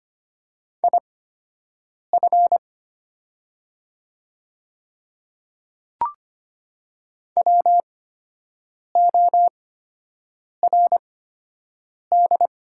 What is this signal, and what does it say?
Morse code, heard as a keyed tone, about 25 wpm, spaced as for 7 wpm, 700 Hz. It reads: IF WORD
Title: 8x Character Spacing - Top 200 Words - 25wpm